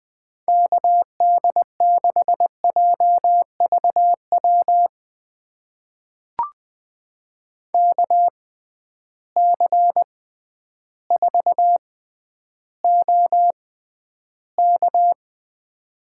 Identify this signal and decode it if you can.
Morse code, heard as a keyed tone, about 20 wpm, spaced as for 11 wpm, 700 Hz. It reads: KD6JVW KC4OK
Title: USA Call Signs - 6x Character Spacing - 20wpm